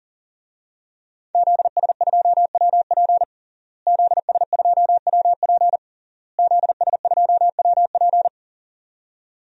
Morse code, heard as a keyed tone, about 40 wpm, 700 Hz. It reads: ZS2WP ZS2WP ZS2WP
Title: DXCC-100 Call Signs - 5 Characters - Repeated 3x - 40wpm